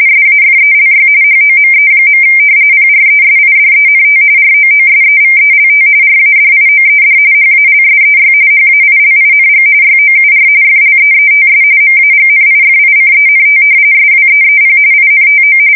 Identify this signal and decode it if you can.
RTTY, about 45.45 baud. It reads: SAFELY HOME. PRESIDENT ROOSEVELT RADIOED HALSEY HAILING YOUR FLEETS MAGNIFICENT SWEEP INTO
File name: fsk170 / battleshipnj